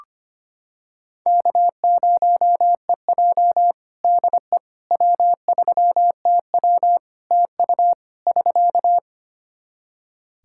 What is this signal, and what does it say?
Morse code, heard as a keyed tone, about 25 wpm, 700 Hz. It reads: K0EJ DE W3TW TU <SK>
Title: Practice QSO - Exchange 4 - Ending - 25wpm